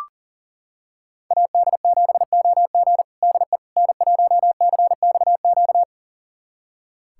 Morse code, heard as a keyed tone, about 40 wpm, 700 Hz. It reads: AD7OG DE N1CXQ
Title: USA Callsign DE Callsign - 5 Characters - 40wpm